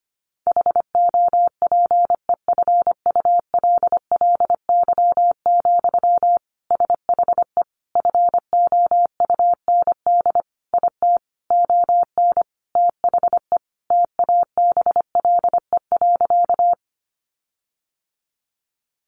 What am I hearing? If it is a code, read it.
Morse code, heard as a keyed tone, about 25 wpm, 700 Hz. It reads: HOPEFULLY, SHE FOUND IT ON THE TABLE.
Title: Sentences from Top 1500 Words 25wpm